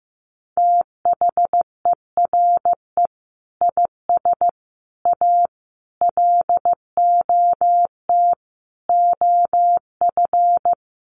Morse code, heard as a keyed tone, about 15 wpm, 700 Hz. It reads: THERE IS A LOT OF
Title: Sets of 5 Words 15wpm